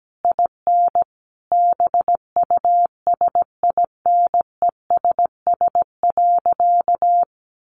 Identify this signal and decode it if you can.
Morse code, heard as a keyed tone, about 17 wpm, 700 Hz. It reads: IN BUSINESS.
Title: Sentences from Top 1300 Words 17wpm